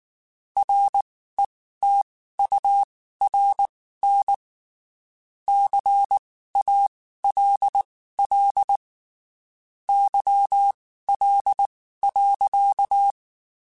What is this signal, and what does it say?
Morse code, heard as a keyed tone, about 19 wpm, spaced as for 14 wpm, 800 Hz. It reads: RETURN CALL YL.
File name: (spaced) / QSOs Spaced-mp3s / WA7QOX 2 @18 sp